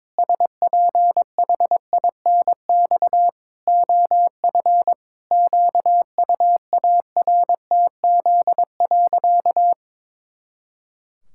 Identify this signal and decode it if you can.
Morse code, heard as a keyed tone, about 22 wpm, 700 Hz. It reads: SPHINX OF QUARTZ.
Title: Pangram Challenge v2 - 22wpm